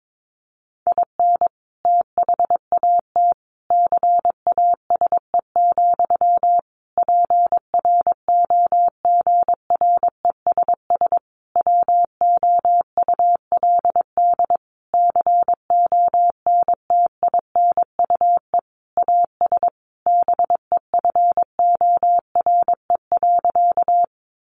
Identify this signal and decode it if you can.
Morse code, heard as a keyed tone, about 22 wpm, 700 Hz. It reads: IN THAT CASE, PROGRESS WOULD CONTINUE AS BEFORE.
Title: Sentences from Top 1400 Words 22wpm